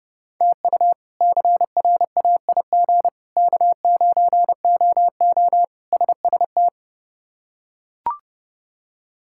Morse code, heard as a keyed tone, about 31 wpm, 700 Hz. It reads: TU CRAIG K9OO SST
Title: K1USN SST QSO Practice - 30wpm